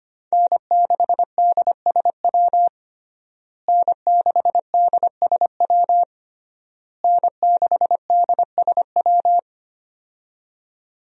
Morse code, heard as a keyed tone, about 25 wpm, 700 Hz. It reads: N6DSW N6DSW N6DSW
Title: USA Call Signs - 5 Characters - Repeated 3x - 25wpm